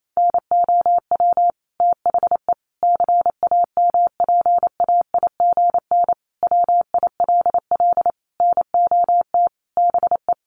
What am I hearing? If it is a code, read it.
Morse code, heard as a keyed tone, about 28 wpm, 700 Hz. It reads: NOW THE CAMPAIGN WILL NOT BE
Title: Sentences from Top 1200 Words 28wpm